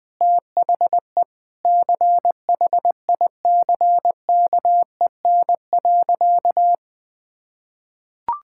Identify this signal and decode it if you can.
Morse code, heard as a keyed tone, about 20 wpm, 700 Hz. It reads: THE CHICKEN.
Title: Sentences from Top 1000 Words - Review Part 5 - 20wpm